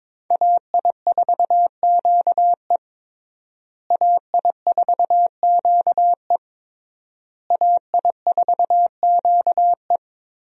Morse code, heard as a keyed tone, about 22 wpm, 690 Hz. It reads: AI4QE AI4QE AI4QE
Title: USA Call Signs - 5 Characters - Repeated 3x - 22wpm